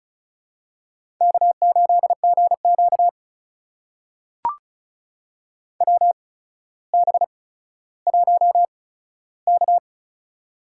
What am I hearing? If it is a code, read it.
Morse code, heard as a keyed tone, about 36 wpm, spaced as for 15 wpm, 700 Hz. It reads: K8GQ WB1K
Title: USA Call Signs - 4 Characters - 8x Character Spacing - 35wpm